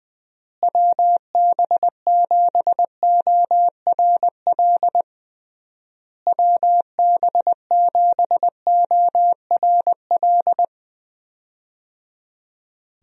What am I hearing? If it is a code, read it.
Morse code, heard as a keyed tone, about 20 wpm, 700 Hz. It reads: WB7ORL WB7ORL
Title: USA Call Signs - 6 Characters - Repeated 3x - 20wpm